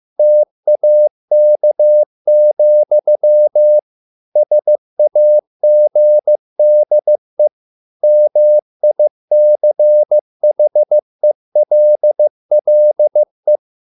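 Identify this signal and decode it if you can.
Morse code, heard as a keyed tone, about 15 wpm, 600 Hz. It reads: TAK, SAGDE MICHELLE
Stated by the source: Morse code at a slow 15 wpm from CW med Gnister.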